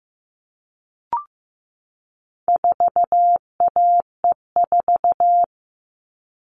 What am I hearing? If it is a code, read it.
Morse code, heard as a keyed tone, about 15 wpm, 700 Hz. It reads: 4AE4
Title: All Cut Numbers - 4-Digit 15wpm